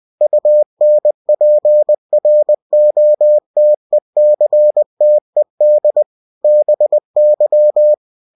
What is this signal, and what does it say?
Morse code, heard as a keyed tone, about 20 wpm, 600 Hz. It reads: UNPROTECTED BY
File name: cw-books / winnie-the-pooh / winnie-20wpm-page-0114